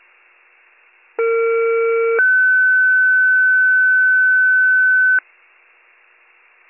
Two-tone page, heard as a one-second tone, then a three-second tone, about 460 then 1500 Hz. It is heard from 1.2 to 5.2 s.